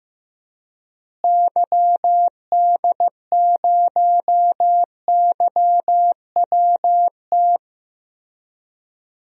Morse code, heard as a keyed tone, about 15 wpm, 700 Hz. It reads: YD0YWT
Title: DXCC-100 Call Signs - 6 Characters - Repeated 3x - 15wpm